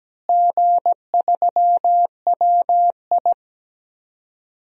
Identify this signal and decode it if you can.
Morse code, heard as a keyed tone, about 17 wpm, 700 Hz. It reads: G3WI
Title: DXCC-100 Call Signs - 4 Characters - Repeated 2x - 17wpm